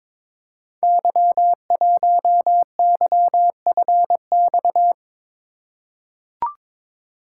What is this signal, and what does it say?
Morse code, heard as a keyed tone, about 22 wpm, 700 Hz. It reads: Y1YFX
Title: Code Groups - 5 Letter:Numbers - 22wpm